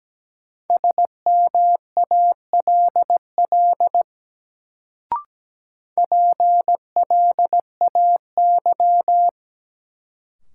Morse code, heard as a keyed tone, about 17 wpm, 700 Hz. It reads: SMALL PLAY
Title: Sentences from Top 200 Words - New Word Drill - 17wpm